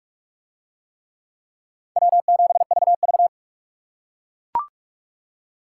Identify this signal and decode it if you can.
Morse code, heard as a keyed tone, about 42 wpm, 700 Hz. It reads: W7VV